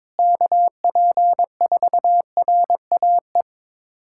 Morse code, heard as a keyed tone, about 21 wpm, 700 Hz. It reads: KP4RAE
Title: USA Callsign DE Callsign - 6 Characters - 22wpm